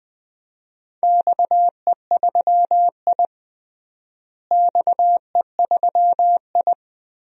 Morse code, heard as a keyed tone, about 20 wpm, 700 Hz. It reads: XE3I XE3I
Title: DXCC-100 Call Signs - 4 Characters - Repeated 2x - 20wpm